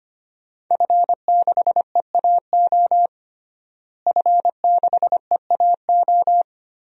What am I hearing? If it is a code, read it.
Morse code, heard as a keyed tone, about 25 wpm, 700 Hz. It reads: F6EAO F6EAO
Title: DXCC-100 Call Signs - 5 Characters - Repeated 2x - 25wpm